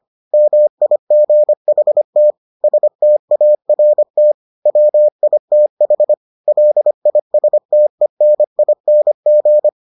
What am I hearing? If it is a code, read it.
Morse code, heard as a keyed tone, about 25 wpm, 600 Hz. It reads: MIGHT START WITH LISTENING